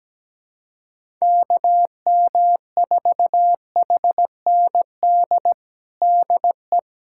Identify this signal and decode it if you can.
Morse code, heard as a keyed tone, about 17 wpm, 700 Hz. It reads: KM4HND DE
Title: USA Callsign DE Callsign 17wpm